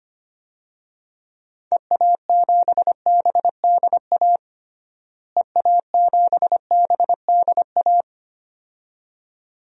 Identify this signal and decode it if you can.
Morse code, heard as a keyed tone, about 25 wpm, 700 Hz. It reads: EA7BDA EA7BDA
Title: DXCC-100 Call Signs - 6 Characters - Repeated 2x - 25wpm